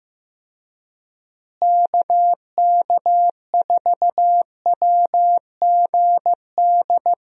Morse code, heard as a keyed tone, about 15 wpm, 700 Hz. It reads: KK4WGD
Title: USA Call Signs - 6 Characters - Repeated 3x - 15wpm